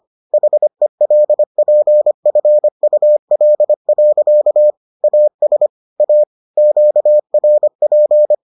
Morse code, heard as a keyed tone, about 25 wpm, 600 Hz. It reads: HELPFUL. AS A QRP